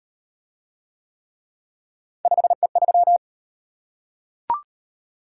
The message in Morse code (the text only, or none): 5E3